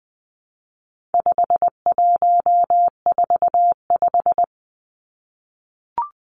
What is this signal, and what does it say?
Morse code, heard as a keyed tone, about 20 wpm, 700 Hz. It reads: 5145
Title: Numbers - 4-Digit 20wpm